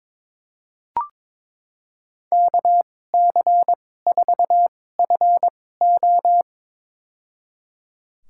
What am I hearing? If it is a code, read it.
Morse code, heard as a keyed tone, about 22 wpm, spaced as for 16 wpm, 700 Hz. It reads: KC4FO
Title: USA Call Signs - 5 Characters - 2x Character Spacing - 22wpm